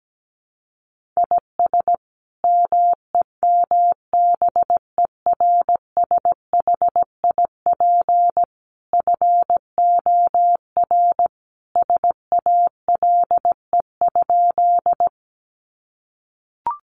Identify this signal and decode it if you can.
Morse code, heard as a keyed tone, about 17 wpm, 700 Hz. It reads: IS MEMBERSHIP FOR SALE?
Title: Sentences from Top 1400 Words 17wpm